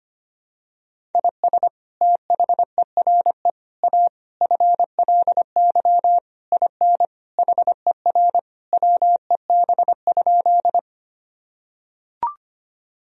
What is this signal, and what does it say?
Morse code, heard as a keyed tone, about 25 wpm, 700 Hz. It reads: IS THERE A FLY IN HER WEB?
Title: Sentences from Top 1000 Words - Review Part 4 - 25wpm